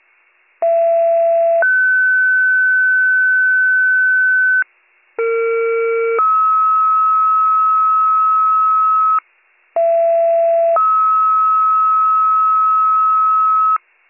Two-tone page, heard as a one-second tone, then a three-second tone, about 670 then 1500 Hz, about 460 then 1290 Hz, about 670 then 1290 Hz, with noise. It is heard from 0.6 to 13.8 s.